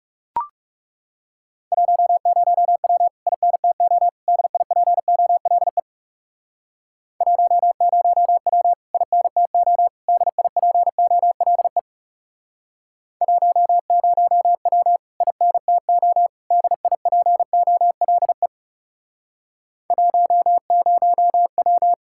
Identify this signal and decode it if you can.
Morse code, heard as a keyed tone, about 37 wpm, 700 Hz. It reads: 10W INTO DIPOLE 10W INTO DIPOLE 10W INTO DIPOLE 10W
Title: Speed-Racing Common QSO Phrases Encore - 30wpm